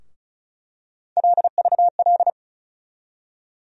Morse code, heard as a keyed tone, about 34 wpm, 700 Hz. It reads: LVL
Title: USA Call Sign Postfixes - 3 Letters 35wpm